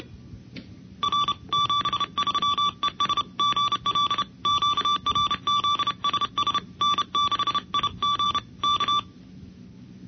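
Morse code, heard as a keyed tone, about 31 wpm, 1160 Hz. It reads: R 73 ES GL QRZ SS N6IG K
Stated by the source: Sweepstakes contest Morse recording from a club contest tape.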